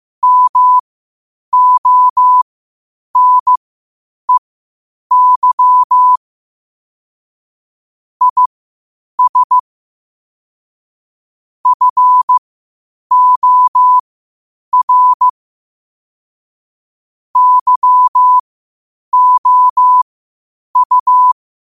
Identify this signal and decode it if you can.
Morse code, heard as a keyed tone, about 15 wpm, spaced as for 8 wpm, 1000 Hz. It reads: MONEY IS FOR YOU
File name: 8wpm